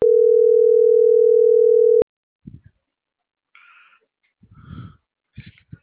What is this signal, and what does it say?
Ringback tone (North America), 440 + 480 Hz.